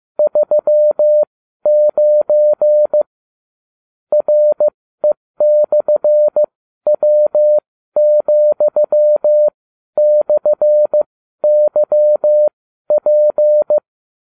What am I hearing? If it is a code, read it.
Morse code, heard as a keyed tone, about 15 wpm, spaced as for 12 wpm, 600 Hz. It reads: 39 RE/W,/YP
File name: cw / ubacw29 2